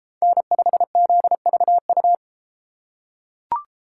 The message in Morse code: N5ZVU

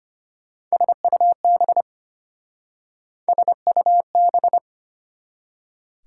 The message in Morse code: SUB SUB

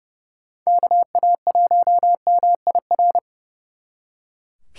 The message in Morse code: KA1MIR

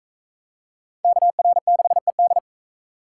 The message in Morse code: KR6ED

